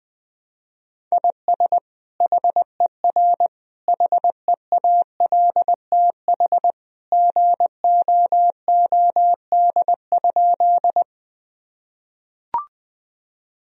Morse code, heard as a keyed tone, about 21 wpm, 700 Hz. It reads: IS HER HEALTH GOOD?
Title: Sentences from Top 1000 Words - Review Part 3 - 20wpm